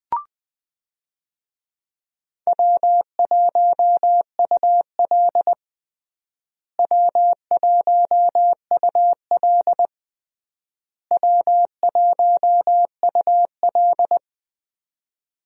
Morse code, heard as a keyed tone, about 20 wpm, 700 Hz. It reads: W1UL W1UL W1UL
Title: USA Call Signs - 4 Characters - Repeated 3x - 20wpm